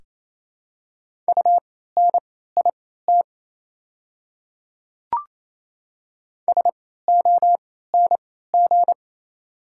Morse code, heard as a keyed tone, about 27 wpm, spaced as for 16 wpm, 700 Hz. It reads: UNIT SONG
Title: 3x Character Spacing - Top 500 Words - 28wpm